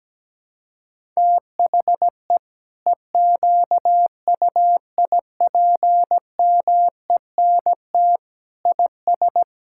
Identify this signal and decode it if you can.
Morse code, heard as a keyed tone, about 17 wpm, 700 Hz. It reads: THE EQUIPMENT IS
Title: Sentences from Top 1000 Words - Review Part 6 - 17wpm